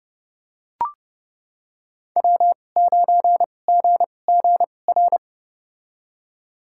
Morse code, heard as a keyed tone, about 30 wpm, spaced as for 22 wpm, 700 Hz. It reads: W9GGR
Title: USA Call Signs - 5 Characters - 2x Character Spacing - 30wpm